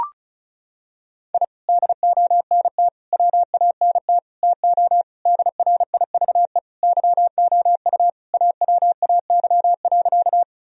Morse code, heard as a keyed tone, about 36 wpm, 700 Hz. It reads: I DONT WANT TO DRIVE YOU AWAY.
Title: Sentences from Top 1000 Words - Review Part 2 - 35wpm